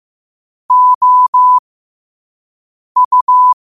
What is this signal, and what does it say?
Morse code, heard as a keyed tone, about 15 wpm, spaced as for 5 wpm, 1000 Hz. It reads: OU
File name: qotdfiles / 5wpm